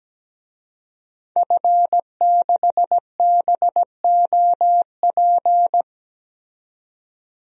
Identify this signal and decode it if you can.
Morse code, heard as a keyed tone, about 17 wpm, 700 Hz. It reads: F6BOP
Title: DXCC-100 Call Signs - 5 Characters - 17wpm